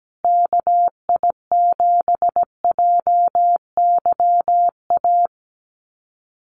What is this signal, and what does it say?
Morse code, heard as a keyed tone, about 17 wpm, 700 Hz. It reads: KI7JYA
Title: USA Call Signs Repeated 3x 17wpm